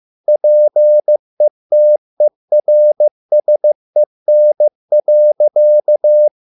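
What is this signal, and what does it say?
Morse code, heard as a keyed tone, about 15 wpm, 600 Hz. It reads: PETERSEN.
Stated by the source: Morse code at a slow 15 wpm from CW med Gnister.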